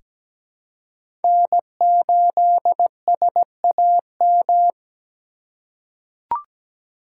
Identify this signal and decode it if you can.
Morse code, heard as a keyed tone, about 17 wpm, 700 Hz. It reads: N8SAM
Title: USA Call Signs - 5 Characters - 2x Character Spacing - 17wpm